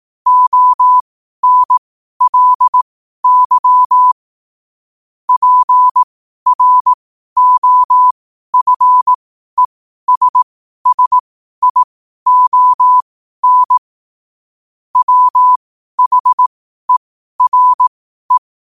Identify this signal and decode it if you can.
Morse code, heard as a keyed tone, about 18 wpm, spaced as for 12 wpm, 1000 Hz. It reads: ONLY PROFESSION WHERE
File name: qotdfiles / 12wpm